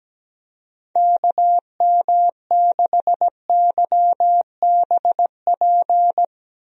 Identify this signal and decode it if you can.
Morse code, heard as a keyed tone, about 17 wpm, 700 Hz. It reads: KM6YBP
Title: USA Call Signs - 6 Characters - Repeated 3x - 17wpm